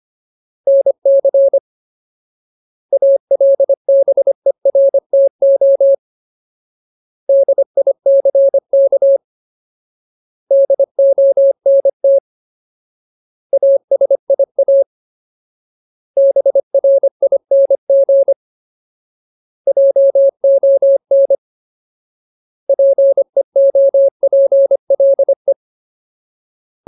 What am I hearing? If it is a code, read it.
Morse code, heard as a keyed tone, about 25 wpm, 550 Hz. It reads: NC ALBERTO DICK DONT ASIA BRING JON PEOPLE
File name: Ultimate Word List / 25WPM Ultimate